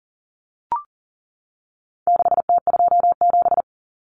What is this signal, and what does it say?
Morse code, heard as a keyed tone, about 40 wpm, 700 Hz. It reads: 6T27
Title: Common Cut Numbers - 4-Digit 40wpm